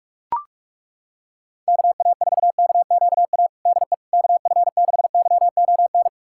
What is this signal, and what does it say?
Morse code, heard as a keyed tone, about 42 wpm, 700 Hz. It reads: KA4KQA DE KF6YON
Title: USA Callsign DE Callsign 45wpm